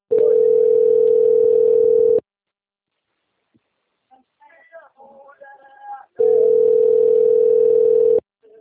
Ringback tone (North America), 440 + 480 Hz.